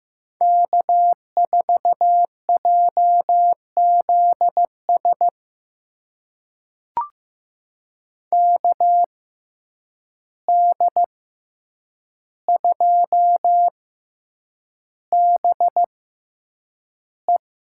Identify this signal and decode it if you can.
Morse code, heard as a keyed tone, about 15 wpm, spaced as for 8 wpm, 700 Hz. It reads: K4JZS KD2BE
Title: USA Call Signs - 5 Characters - 6x Character Spacing - 15wpm